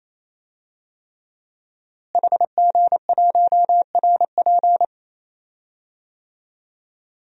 Morse code, heard as a keyed tone, about 28 wpm, 700 Hz. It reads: HG1RP
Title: DXCC-100 Call Signs - 5 Characters - Repeated 3x - 28wpm